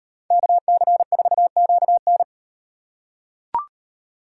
Morse code, heard as a keyed tone, about 36 wpm, 700 Hz. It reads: KC4QN